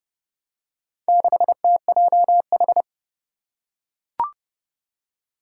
Morse code, heard as a keyed tone, about 30 wpm, 700 Hz. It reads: 6TJH
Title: Four Challenging Letter-Number Combinations 30wpm